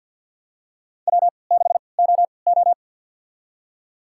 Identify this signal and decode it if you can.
Morse code, heard as a keyed tone, about 51 wpm, spaced as for 28 wpm, 700 Hz. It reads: W6OO
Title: USA Call Signs - 4 Characters - 3x Character Spacing - 50wpm